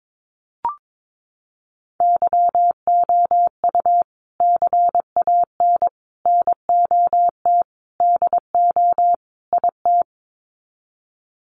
Morse code, heard as a keyed tone, about 22 wpm, 700 Hz. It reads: YOU CAN NOT DO IT
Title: Sets of 5 Words Encore2 22wpm